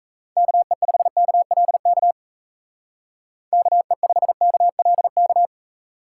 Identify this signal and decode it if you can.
Morse code, heard as a keyed tone, about 33 wpm, 700 Hz. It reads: KE5KLK KE5KLK
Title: Speed Racing - USA 2x3 Call Signs 28wpm